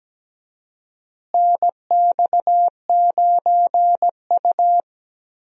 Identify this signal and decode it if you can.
Morse code, heard as a keyed tone, about 17 wpm, 700 Hz. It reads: NX9U
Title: USA Call Signs - 4 Characters - Repeated 3x - 17wpm